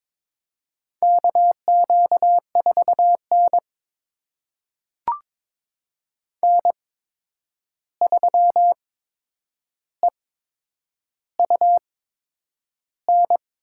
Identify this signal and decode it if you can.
Morse code, heard as a keyed tone, about 22 wpm, spaced as for 9 wpm, 700 Hz. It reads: KQ4N N3EUN
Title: USA Call Signs - 8x Character Spacing - 22wpm